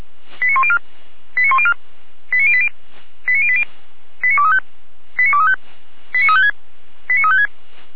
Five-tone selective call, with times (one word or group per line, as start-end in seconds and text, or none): 0.4-0.8 89104
1.4-1.7 89104
2.3-2.7 89080
3.3-3.6 89080
4.2-4.6 89235
5.2-5.6 89235
6.1-6.5 89356
7.1-7.5 89356